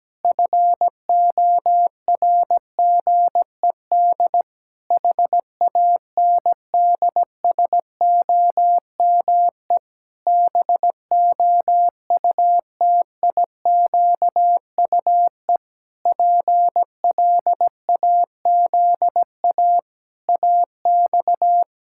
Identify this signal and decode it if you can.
Morse code, heard as a keyed tone, about 17 wpm, 700 Hz. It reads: FORGED HANDSOME BOUTIQUE PLAZA AX
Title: Pangram Challenge v2 - 17wpm